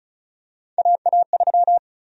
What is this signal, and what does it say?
Morse code, heard as a keyed tone, about 36 wpm, 700 Hz. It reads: AA3
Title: USA Call Sign Prefixes - 35wpm